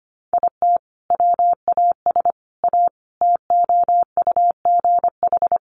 Morse code, heard as a keyed tone, about 25 wpm, 700 Hz. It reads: IT WAS A TOUGH
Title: Sentences from Top 1200 Words 25wpm